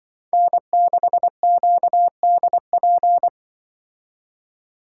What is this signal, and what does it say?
Morse code, heard as a keyed tone, about 24 wpm, 700 Hz. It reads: N6QDP
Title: Speed Racing - USA 1x3 Call Signs 20wpm